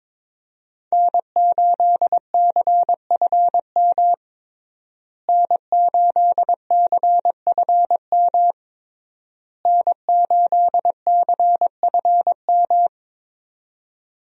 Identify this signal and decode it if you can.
Morse code, heard as a keyed tone, about 22 wpm, 700 Hz. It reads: N8CFM N8CFM N8CFM
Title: USA Call Signs - 5 Characters - Repeated 3x - 22wpm